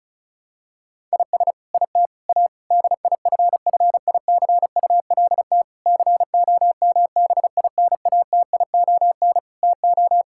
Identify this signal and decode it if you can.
Morse code, heard as a keyed tone, about 36 wpm, 700 Hz. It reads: IS IT A DIFFICULT COMBINATION TO
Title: Sentences from Top 1000 Words - Review Part 6 - 35wpm